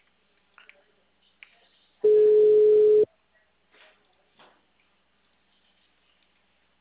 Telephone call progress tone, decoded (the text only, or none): ringback tone